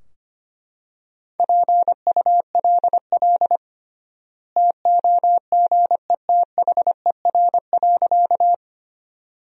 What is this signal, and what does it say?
Morse code, heard as a keyed tone, about 25 wpm, 700 Hz. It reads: PULL TOGETHER.
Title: 3x Word-Spacing - Sentences from Top 400 Words - 25wpm